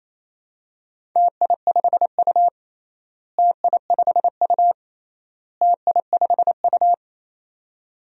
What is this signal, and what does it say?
Morse code, heard as a keyed tone, about 28 wpm, 700 Hz. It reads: TI5U TI5U TI5U
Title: DXCC-100 Call Signs - 4 Characters - Repeated 3x - 28wpm